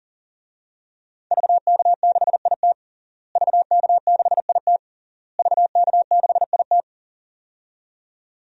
Morse code, heard as a keyed tone, about 40 wpm, 700 Hz. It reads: VK6IT VK6IT VK6IT
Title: DXCC-100 Call Signs - 5 Characters - Repeated 3x - 40wpm